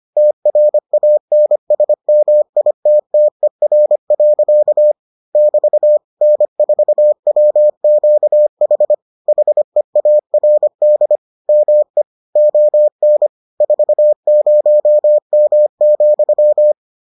Morse code, heard as a keyed tone, about 25 wpm, 600 Hz. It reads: TRANSMITTER. <BT> N4WQH HEARD ME ON 40M,